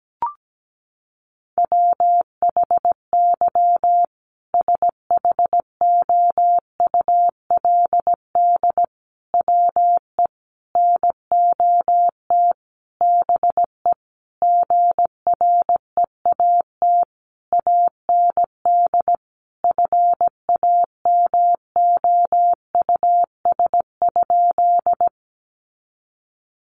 Morse code, heard as a keyed tone, about 17 wpm, 700 Hz. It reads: WHY SHOULD WE NOT BE GREAT AND FAMOUS?
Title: Sentences from Top 1500 Words 17wpm